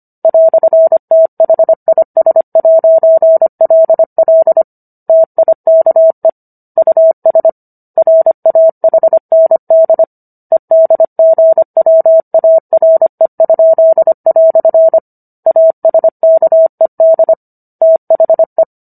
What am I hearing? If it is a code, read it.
Morse code, heard as a keyed tone, about 25 wpm, 650 Hz. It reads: "THIS'LL TIKE US RAHND EDGWARE?" ASKED THE